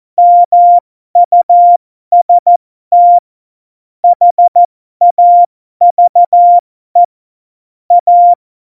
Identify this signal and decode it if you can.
Morse code, heard as a keyed tone, about 14 wpm, spaced as for 12 wpm, 700 Hz. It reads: MUST HAVE A
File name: Morsekurs / Witze